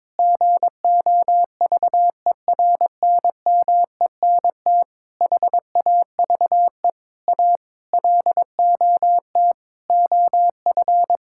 Morse code, heard as a keyed tone, about 22 wpm, 700 Hz. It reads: GOVERNMENT HAVE A LOT OF
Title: Sentences from Top 1000 Words - Review Part 5 - 22wpm